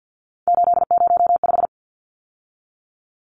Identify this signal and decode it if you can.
Morse code, heard as a keyed tone, about 50 wpm, 700 Hz. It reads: 805
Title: Numbers - 3-Digit 50wpm